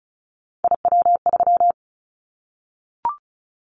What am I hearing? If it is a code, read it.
Morse code, heard as a keyed tone, about 36 wpm, 690 Hz. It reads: IW3
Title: Callsign Prefixes for Top 25 DXCC - 35wpm